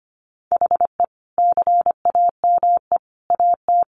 Morse code, heard as a keyed tone, about 25 wpm, 700 Hz. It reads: HE CAME AT
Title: Sentences from Top 300 Words 25wpm